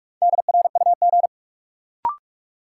Morse code, heard as a keyed tone, about 46 wpm, 700 Hz. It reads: DRUG
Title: Sets of 5 Words - New Word Drill - 45wpm